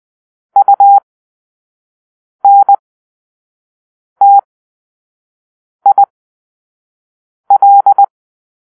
Morse code, heard as a keyed tone, about 20 wpm, spaced as for 5 wpm, 800 Hz. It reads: UNTIL